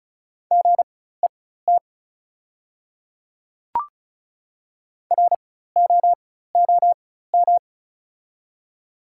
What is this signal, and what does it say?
Morse code, heard as a keyed tone, about 36 wpm, spaced as for 16 wpm, 700 Hz. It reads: GET ROOM
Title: 4x Character Spacing - Top 300 Words - 35wpm